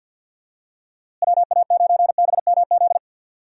Morse code, heard as a keyed tone, about 53 wpm, 700 Hz. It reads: WA9BMZ
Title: USA Call Signs - 6 Characters - Repeated 2x - 50wpm